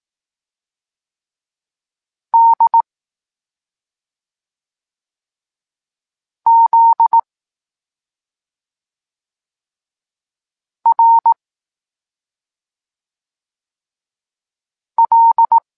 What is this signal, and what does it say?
Morse code, heard as a keyed tone, about 18 wpm, 920 Hz. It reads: D Z R L
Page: Tekst 23 – Doskonalenie odbioru poznanych liter